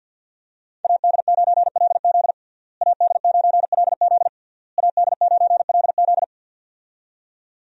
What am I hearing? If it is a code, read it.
Morse code, heard as a keyed tone, about 47 wpm, 700 Hz. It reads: AD9LZ AD9LZ AD9LZ
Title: USA Call Signs - 5 Characters - Repeated 3x - 50wpm